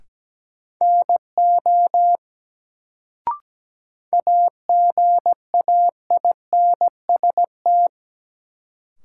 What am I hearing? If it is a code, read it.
Morse code, heard as a keyed tone, about 17 wpm, 700 Hz. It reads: NO AGAINST
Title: Sets of 2 Words - New Word Drill - 17wpm